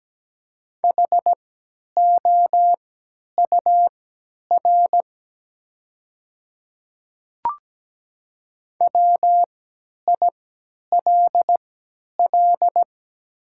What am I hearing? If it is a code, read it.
Morse code, heard as a keyed tone, about 17 wpm, spaced as for 10 wpm, 700 Hz. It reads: HOUR WILL
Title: 3x Character Spacing - Top 500 Words - 17wpm